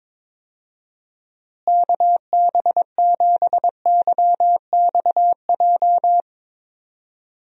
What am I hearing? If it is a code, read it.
Morse code, heard as a keyed tone, about 22 wpm, 700 Hz. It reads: KB7YXJ
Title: USA Call Signs - 6 Characters - Repeated 2x - 22wpm